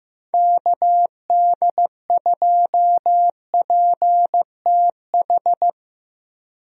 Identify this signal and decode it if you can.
Morse code, heard as a keyed tone, about 15 wpm, 700 Hz. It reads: KD2PTH
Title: USA Call Signs - 6 Characters - Repeated 3x - 15wpm